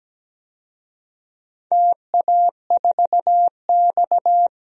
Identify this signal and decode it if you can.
Morse code, heard as a keyed tone, about 17 wpm, 700 Hz. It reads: TA4X
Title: DXCC-100 Call Signs - 4 Characters - Repeated 3x - 17wpm